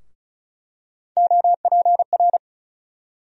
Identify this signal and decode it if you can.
Morse code, heard as a keyed tone, about 35 wpm, 700 Hz. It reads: OPR
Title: USA Call Sign Postfixes - 3 Letters 35wpm